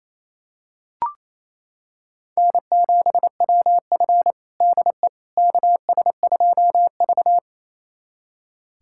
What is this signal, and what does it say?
Morse code, heard as a keyed tone, about 28 wpm, 700 Hz. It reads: N7WF DE KS2V
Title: USA Callsign DE Callsign - 4 Characters - 28wpm